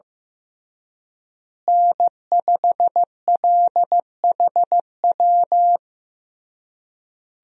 Morse code, heard as a keyed tone, about 15 wpm, 700 Hz. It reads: N5LHW
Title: USA Call Signs - 5 Characters - Repeated 3x - 15wpm